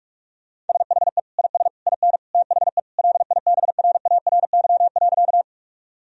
Morse code, heard as a keyed tone, about 46 wpm, 700 Hz. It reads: SHE IS IN THE LIBRARY.
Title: Sentences from Top 1000 Words - Review Part 4 - 45wpm